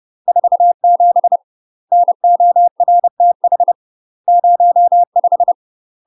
Morse code, heard as a keyed tone, about 30 wpm, 700 Hz. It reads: 47 NORTH 05
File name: UK MASTER QSOs TRAINING/30WPM